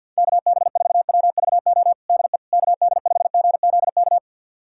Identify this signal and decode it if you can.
Morse code, heard as a keyed tone, about 52 wpm, 690 Hz. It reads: KB4WVO DE KD5GZK
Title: USA Callsign DE Callsign 50wpm